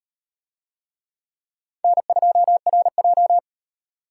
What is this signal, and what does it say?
Morse code, heard as a keyed tone, about 38 wpm, 700 Hz. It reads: N2RJ